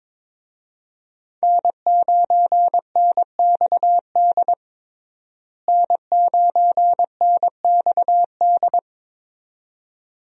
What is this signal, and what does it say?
Morse code, heard as a keyed tone, about 22 wpm, 700 Hz. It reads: N9NXD N9NXD
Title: USA Call Signs - 5 Characters - Repeated 2x - 22wpm